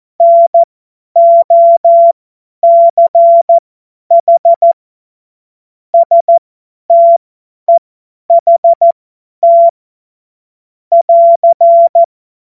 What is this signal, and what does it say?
Morse code, heard as a keyed tone, about 14 wpm, spaced as for 10 wpm, 680 Hz. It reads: NOCH STEHT <AR>
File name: Morsekurs / Witze / Witz 15. November 2022 50 F70